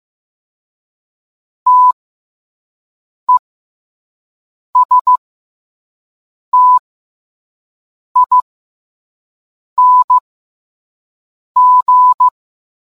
Morse code, heard as a keyed tone, about 15 wpm, spaced as for 5 wpm, 1000 Hz. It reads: TESTING